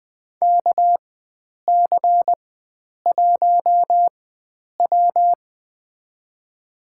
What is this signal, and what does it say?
Morse code, heard as a keyed tone, about 20 wpm, spaced as for 9 wpm, 700 Hz. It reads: KC1W
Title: USA Call Signs - 4 Characters - 4x Character Spacing - 20wpm